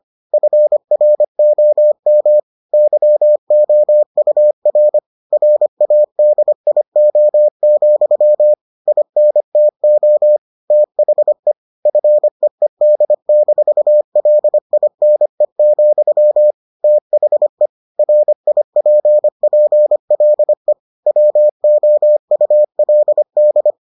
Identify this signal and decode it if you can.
Morse code, heard as a keyed tone, about 25 wpm, 600 Hz. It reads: FROM YOUR RADIO, INTO THE FEED-LINE, THE RIPPLE WOULD